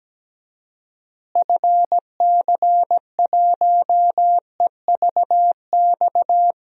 Morse code, heard as a keyed tone, about 17 wpm, 700 Hz. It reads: FC1EVX
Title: DXCC-100 Call Signs - 6 Characters - Repeated 3x - 17wpm